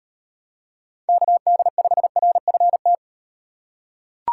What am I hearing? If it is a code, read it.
Morse code, heard as a keyed tone, about 35 wpm, 700 Hz. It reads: KD5RFT